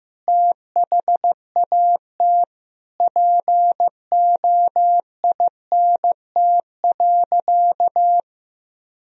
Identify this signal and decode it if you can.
Morse code, heard as a keyed tone, about 15 wpm, 700 Hz. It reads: THAT POINT.
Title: Sentences from Top 1000 Words - Review Part 1 - 15wpm